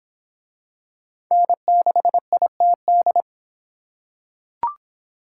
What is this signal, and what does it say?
Morse code, heard as a keyed tone, about 26 wpm, 700 Hz. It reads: N6ITD